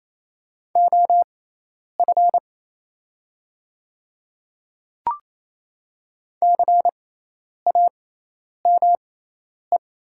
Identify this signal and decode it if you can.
Morse code, heard as a keyed tone, about 28 wpm, spaced as for 10 wpm, 700 Hz. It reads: OF CAME
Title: 6x Character Spacing - Top 200 Words - 28wpm